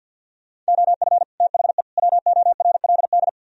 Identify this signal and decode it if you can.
Morse code, heard as a keyed tone, about 60 wpm, 700 Hz. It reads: OF THE WORLD
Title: Speed-Racing - Sets of 3 Words 50wpm